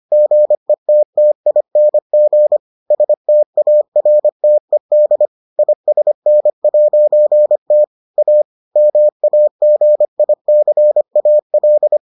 Morse code, heard as a keyed tone, about 25 wpm, 600 Hz. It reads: GETTING STARTED ISN'T A MAGICAL